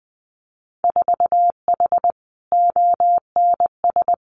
Morse code, heard as a keyed tone, about 20 wpm, 700 Hz. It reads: 4H ONS
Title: Winter Field Day Exchanges Encore 20wpm